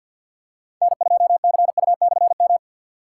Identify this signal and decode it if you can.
Morse code, heard as a keyed tone, about 46 wpm, 700 Hz. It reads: N2CU/M
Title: K1USN SST QSO Practice - 50wpm